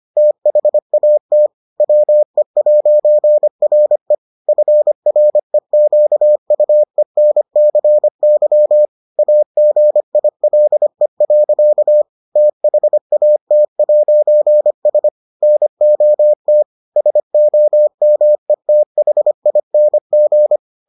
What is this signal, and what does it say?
Morse code, heard as a keyed tone, about 25 wpm, 600 Hz. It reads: THAT WE'RE FREQUENCY AGILE. THAT'S NOT SOMETHING